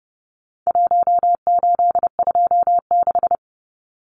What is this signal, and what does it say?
Morse code, heard as a keyed tone, about 30 wpm, 700 Hz. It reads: 1826
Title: Numbers - Contest Style - 4-Digit 30wpm